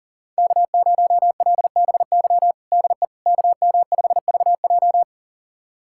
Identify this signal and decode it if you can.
Morse code, heard as a keyed tone, about 40 wpm, 700 Hz. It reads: K0LBY DE KM5VJ
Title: USA Callsign DE Callsign - 5 Characters - 40wpm